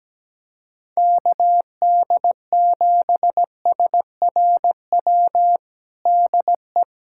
Morse code, heard as a keyed tone, about 17 wpm, 700 Hz. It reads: KD7SRW DE
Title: USA Callsign DE Callsign - 6 Characters - 17wpm